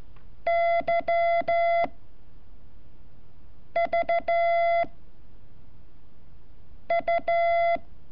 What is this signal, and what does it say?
Morse code, heard as a keyed tone, about 14 wpm, 680 Hz. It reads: Y V U
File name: curso / ejercicio 28